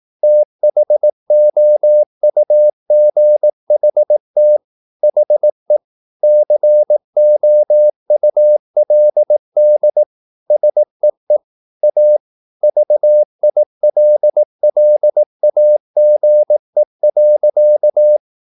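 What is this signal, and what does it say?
Morse code, heard as a keyed tone, about 18 wpm, 600 Hz. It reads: THOUGHT HE COULD SEE A VILLAGE.
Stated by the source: Morse practice at 18 wpm, English text, from CW med Gnister.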